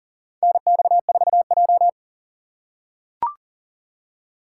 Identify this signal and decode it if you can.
Morse code, heard as a keyed tone, about 40 wpm, 690 Hz. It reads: NX4J
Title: USA Call Signs - 4 Characters - 2x Character Spacing - 40wpm